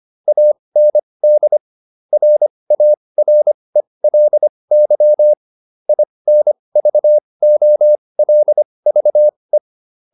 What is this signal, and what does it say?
Morse code, heard as a keyed tone, about 25 wpm, spaced as for 20 wpm, 600 Hz. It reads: AND RARELY INVOLVE
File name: SS306 20